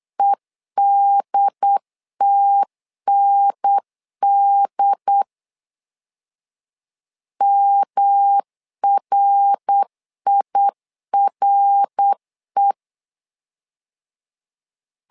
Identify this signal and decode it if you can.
Morse code, heard as a keyed tone, about 8 wpm, 800 Hz. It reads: EDTND MRIRE
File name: LEZIONI / 3 MWRO / 3 MWR 8wpm 2444